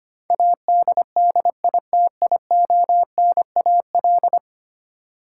Morse code, heard as a keyed tone, about 25 wpm, 700 Hz. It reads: ADDITIONAL
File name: Ten Letter Words - 25wpm